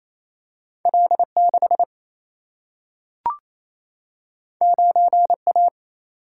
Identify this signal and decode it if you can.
Morse code, heard as a keyed tone, about 28 wpm, 700 Hz. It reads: L 6 9 A
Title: Single-Letter Single-Number 28wpm